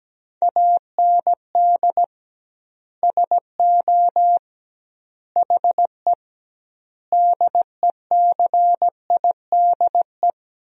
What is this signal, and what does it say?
Morse code, heard as a keyed tone, about 17 wpm, 700 Hz. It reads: AND SO HE DECIDE
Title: 2x Word-Spacing - Sentences from Top 500 Words - 17wpm